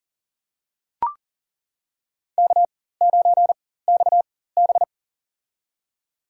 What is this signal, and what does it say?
Morse code, heard as a keyed tone, about 40 wpm, spaced as for 19 wpm, 700 Hz. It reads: K9XB
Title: USA Call Signs - 4 Characters - 4x Character Spacing - 40wpm